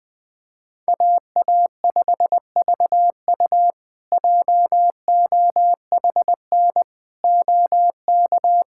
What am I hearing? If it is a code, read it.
Morse code, heard as a keyed tone, about 20 wpm, 700 Hz. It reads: AA5VU JOHN OK
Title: K1USN SST QSO Practice - 20wpm